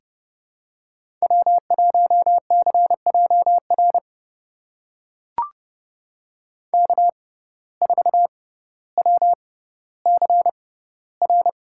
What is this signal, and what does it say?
Morse code, heard as a keyed tone, about 30 wpm, spaced as for 15 wpm, 700 Hz. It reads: W1CJR K4WCR
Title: USA Call Signs - 5 Characters - 6x Character Spacing - 30wpm